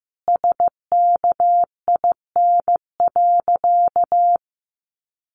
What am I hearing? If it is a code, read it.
Morse code, heard as a keyed tone, about 15 wpm, 700 Hz. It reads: SKIN.
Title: Sentences from Top 1300 Words 15wpm